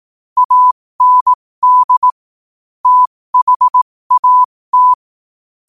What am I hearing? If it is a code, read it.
Morse code, heard as a keyed tone, about 18 wpm, spaced as for 15 wpm, 1000 Hz. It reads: AND THAT
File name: qotdfiles / 15wpm